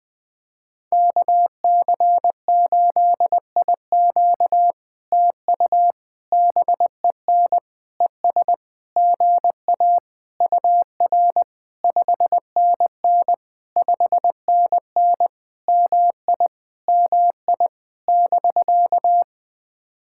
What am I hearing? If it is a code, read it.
Morse code, heard as a keyed tone, about 20 wpm, 700 Hz. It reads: KC8IQ TU BEN ES GA UR 5NN 5NN MI MI <BK>
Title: POTA QSO Practice - 20wpm